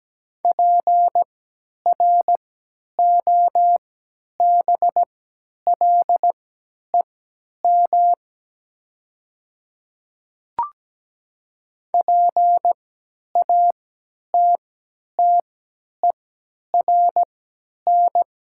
Morse code, heard as a keyed tone, about 17 wpm, spaced as for 10 wpm, 700 Hz. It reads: PROBLEM PATTERN
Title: 3x Character Spacing - Top 500 Words - 17wpm